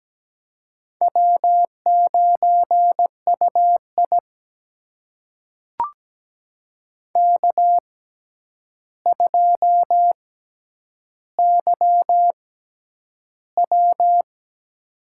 Morse code, heard as a keyed tone, about 17 wpm, spaced as for 9 wpm, 700 Hz. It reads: W9UI K2YW
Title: USA Call Signs - 4 Characters - 6x Character Spacing - 17wpm